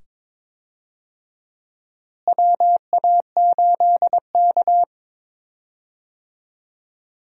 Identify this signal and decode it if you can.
Morse code, heard as a keyed tone, about 22 wpm, 700 Hz. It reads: WA8K
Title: USA Call Signs - 4 Characters - Repeated 2x - 22wpm